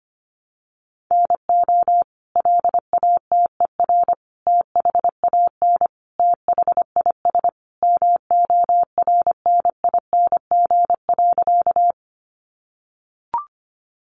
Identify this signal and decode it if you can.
Morse code, heard as a keyed tone, about 25 wpm, 700 Hz. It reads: NO LATER THAN THIS MORNING.
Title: Sentences from Top 700 Words 25wpm